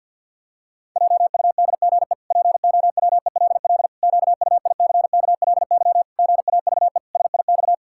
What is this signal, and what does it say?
Morse code, heard as a keyed tone, about 52 wpm, 700 Hz. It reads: JUDGE POWELL QUICKLY GAVE SIX
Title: Pangram Challenge v2 - 50wpm